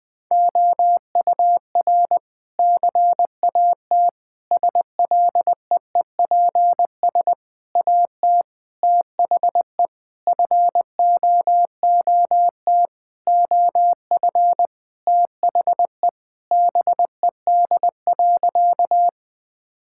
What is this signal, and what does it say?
Morse code, heard as a keyed tone, about 19 wpm, 700 Hz. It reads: OUR CAT SLEEPS AT THE FOOT OF THE BED.
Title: Sentences with 5-Word Phrases - 20wpm